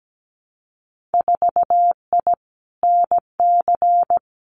Morse code, heard as a keyed tone, about 17 wpm, 700 Hz. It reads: 4I NC
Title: Winter Field Day Exchanges 17wpm